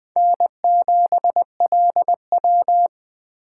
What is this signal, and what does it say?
Morse code, heard as a keyed tone, about 20 wpm, 700 Hz. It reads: N7LW
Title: Speed Racing - USA 1x2 Call Signs 17wpm